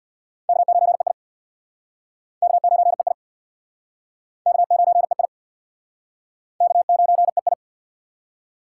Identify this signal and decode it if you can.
Morse code, heard as a keyed tone, about 63 wpm, 700 Hz. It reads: K9EI K9EI K9EI K9EI